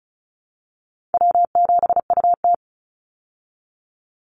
Morse code, heard as a keyed tone, about 34 wpm, 700 Hz. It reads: W7UT
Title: Call Signs Repeated 2x - 35wpm